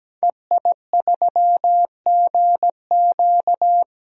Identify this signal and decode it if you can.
Morse code, heard as a keyed tone, about 17 wpm, 700 Hz. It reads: EI3GQ
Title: DXCC-100 Call Signs - 5 Characters - 17wpm